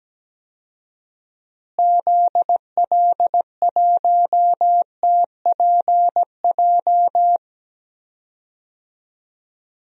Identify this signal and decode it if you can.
Morse code, heard as a keyed tone, about 17 wpm, 700 Hz. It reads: ZL1TPJ
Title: DXCC-100 Call Signs - 6 Characters - Repeated 3x - 17wpm